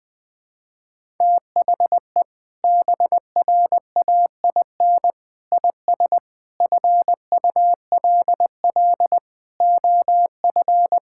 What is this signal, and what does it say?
Morse code, heard as a keyed tone, about 20 wpm, 700 Hz. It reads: THE BRAIN IS FULL OF
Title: Sentences from Top 1000 Words - Review Part 6 - 20wpm